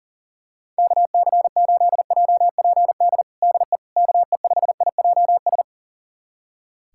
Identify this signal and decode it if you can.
Morse code, heard as a keyed tone, about 40 wpm, 700 Hz. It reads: KC8JPD DE KE5IJS
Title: USA Callsign DE Callsign - 6 Characters - 40wpm